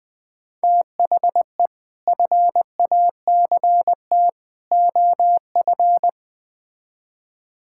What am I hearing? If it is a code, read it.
Morse code, heard as a keyed tone, about 20 wpm, 700 Hz. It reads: THE FACT OF
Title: Build A Phrase - 5-Word Phrases - 20wpm